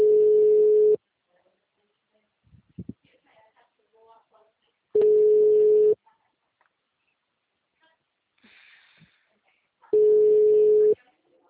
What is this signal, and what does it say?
Ringback tone (Europe, 425 Hz).
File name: out- -2019